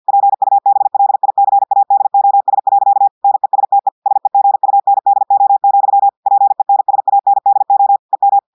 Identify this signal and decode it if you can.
Morse code, heard as a keyed tone, about 50 wpm, 800 Hz. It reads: PUBLICADOS. NESTE SEGUNDO, PENSANDO EM
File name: audios / portugais / portugais texte en clair 1 vitesse 50 mots minutes